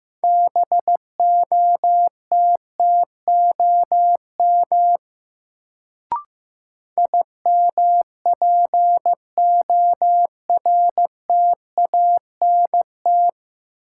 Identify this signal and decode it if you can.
Morse code, heard as a keyed tone, about 15 wpm, 700 Hz. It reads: BOTTOM IMPORTANT
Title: Sets of 3 Words - New Word Drill - 15wpm